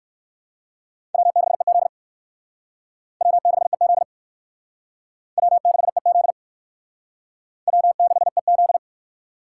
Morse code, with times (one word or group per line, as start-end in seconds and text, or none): 1.1-1.9 W6EZ
3.2-4.0 W6EZ
5.4-6.3 W6EZ
7.7-8.8 W6EZ